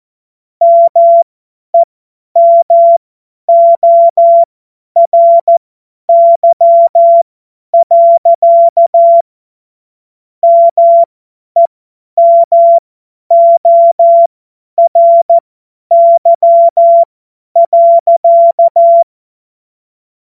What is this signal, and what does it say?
Morse code, heard as a keyed tone, about 14 wpm, spaced as for 10 wpm, 680 Hz. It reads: MEMORY. MEMORY.
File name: Morsekurs / Witze / Witz 14. Dezember 2021, 50 F70